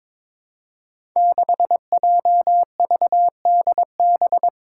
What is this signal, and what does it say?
Morse code, heard as a keyed tone, about 22 wpm, 700 Hz. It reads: 6JVDB
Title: Five Challenging Letter-Number Combinations 22wpm